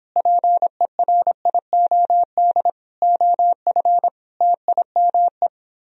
Morse code, heard as a keyed tone, about 29 wpm, 700 Hz. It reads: PERIOD OF TIME